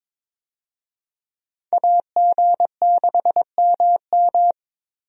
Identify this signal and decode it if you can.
Morse code, heard as a keyed tone, about 21 wpm, 700 Hz. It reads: AG6MM
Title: USA Call Signs - 5 Characters - Repeated 3x - 22wpm